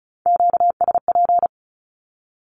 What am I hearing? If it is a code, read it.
Morse code, heard as a keyed tone, about 34 wpm, 700 Hz. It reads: QSP
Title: Q-Codes 35wpm